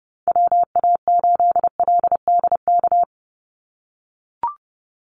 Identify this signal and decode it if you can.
Morse code, heard as a keyed tone, about 30 wpm, 700 Hz. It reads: WA8LDK
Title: USA Call Signs 30wpm